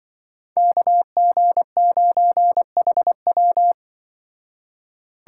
Morse code, heard as a keyed tone, about 24 wpm, 700 Hz. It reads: KG9HW
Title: Speed Racing - USA 2x2 Call Signs 20wpm